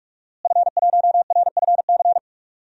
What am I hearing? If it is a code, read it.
Morse code, heard as a keyed tone, about 44 wpm, 700 Hz. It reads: F1RFC
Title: DXCC-100 Call Signs - 5 Characters - Repeated 3x - 45wpm